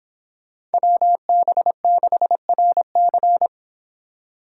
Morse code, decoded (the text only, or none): WB6RC